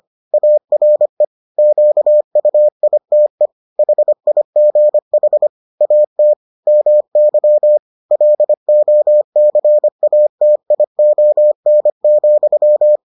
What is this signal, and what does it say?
Morse code, heard as a keyed tone, about 25 wpm, 600 Hz. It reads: ARE QUITE HIGH AT MY LOCATION,